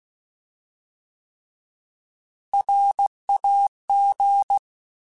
Morse code, heard as a keyed tone, about 16 wpm, 800 Hz. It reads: RAG
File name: Alphabetical / Alphabetical mp3s / Rs @15